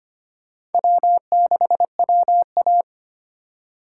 Morse code, heard as a keyed tone, about 34 wpm, 700 Hz. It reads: W6WA